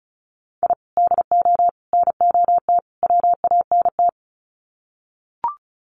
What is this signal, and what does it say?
Morse code, heard as a keyed tone, about 34 wpm, 700 Hz. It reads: I DO NOT WANT
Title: Sets of 4 Words 35wpm